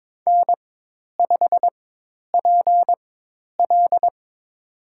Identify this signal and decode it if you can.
Morse code, heard as a keyed tone, about 22 wpm, spaced as for 10 wpm, 700 Hz. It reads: N5PL
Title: USA Call Signs - 4 Characters - 4x Character Spacing - 22wpm